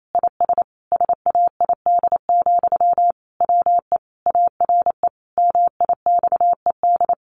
Morse code, heard as a keyed tone, about 28 wpm, 700 Hz. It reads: IS SAID, WE ARE MIXED
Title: Sentences from Top 1500 Words 28wpm